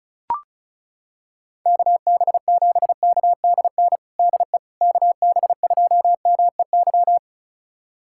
Morse code, heard as a keyed tone, about 34 wpm, 700 Hz. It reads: KB7KDN DE KB2MEY
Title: USA Callsign DE Callsign - 6 Characters - 35wpm